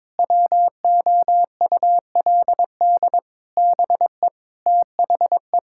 Morse code, heard as a keyed tone, about 22 wpm, 700 Hz. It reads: WOULD BE THE
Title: Sentences from Top 1000 Words - Review Part 5 - 22wpm